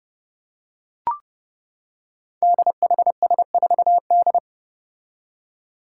Morse code, heard as a keyed tone, about 30 wpm, 700 Hz. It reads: DHS4D
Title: Five Challenging Letter-Number Combinations 30wpm